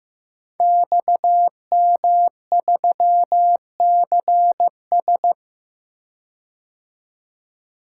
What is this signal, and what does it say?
Morse code, heard as a keyed tone, about 15 wpm, 700 Hz. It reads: XM3CS
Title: Code Groups - 5 Letter:Numbers - 15wpm